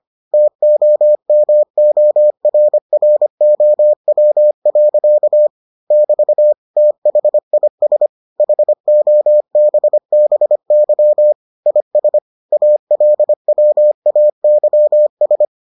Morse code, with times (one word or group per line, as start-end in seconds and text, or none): 0.3-5.5 TOMORROW.
5.9-6.5 <BT>
6.8-8.1 THIS
8.4-11.3 HOBBY
11.7-12.2 IS
12.5-15.4 ALWAYS